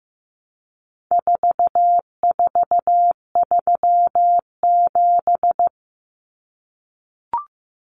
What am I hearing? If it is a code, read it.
Morse code, heard as a keyed tone, about 15 wpm, 700 Hz. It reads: 4437
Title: Numbers - 4-Digit 15wpm